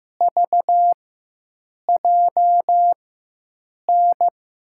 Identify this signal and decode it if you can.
Morse code, heard as a keyed tone, about 15 wpm, spaced as for 7 wpm, 700 Hz. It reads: VJN
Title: USA Call Signs - 6 Characters - 4x Character Spacing - 15wpm